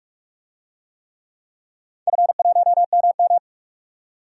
Morse code, heard as a keyed tone, about 42 wpm, 700 Hz. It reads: F1MM
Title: DXCC-100 Call Signs - 4 Characters - 45wpm